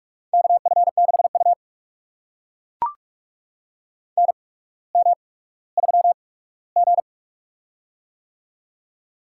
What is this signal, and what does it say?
Morse code, heard as a keyed tone, about 42 wpm, spaced as for 19 wpm, 700 Hz. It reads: KF6U NM3G
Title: USA Call Signs - 4 Characters - 8x Character Spacing - 45wpm